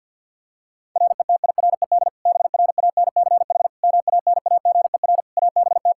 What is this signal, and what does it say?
Morse code, heard as a keyed tone, about 52 wpm, 700 Hz. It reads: RETIRED BRANCH MANAGER ABT
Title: Practice QSO - Exchange 3 - More Info - 50wpm